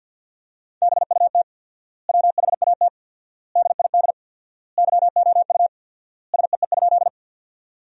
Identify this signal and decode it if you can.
Morse code, heard as a keyed tone, about 52 wpm, 700 Hz. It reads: BUT WHAT DID YOU SEE?
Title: 4x Word-Spacing - Sentences from Top 100 Words - 50wpm